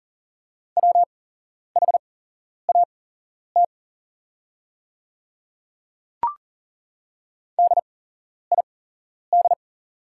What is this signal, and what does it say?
Morse code, heard as a keyed tone, about 40 wpm, spaced as for 11 wpm, 700 Hz. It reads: WHAT DID
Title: 8x Character Spacing - Top 100 Words - 40wpm